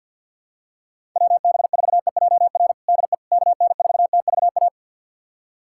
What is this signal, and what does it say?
Morse code, heard as a keyed tone, about 47 wpm, 700 Hz. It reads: WB4EJR DE KN4TVA
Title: USA Callsign DE Callsign 50wpm